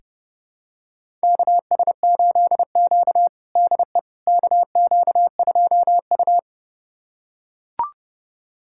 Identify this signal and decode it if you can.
Morse code, heard as a keyed tone, about 32 wpm, 700 Hz. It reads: KS8Q DE KQ2U
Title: USA Callsign DE Callsign - 4 Characters - 30wpm